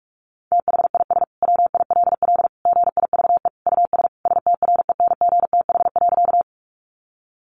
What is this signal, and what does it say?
Morse code, heard as a keyed tone, about 44 wpm, 700 Hz. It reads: THIS WILL GIVE US STRENGTH.
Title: Sentences from Top 1300 Words 45wpm